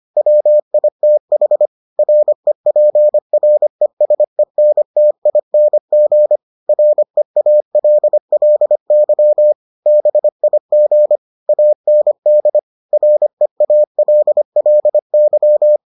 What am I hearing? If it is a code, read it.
Morse code, heard as a keyed tone, about 25 wpm, 600 Hz. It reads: WITH REPRESENTING REALLY BIG AND REALLY